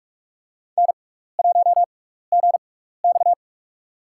Morse code, heard as a keyed tone, about 46 wpm, spaced as for 16 wpm, 700 Hz. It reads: N1GX
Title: USA Call Signs - 6x Character Spacing - 45wpm